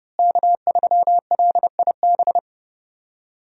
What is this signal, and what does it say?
Morse code, heard as a keyed tone, about 30 wpm, 700 Hz. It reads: K3LIB
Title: USA Call Signs - 5 Characters - Repeated 3x - 30wpm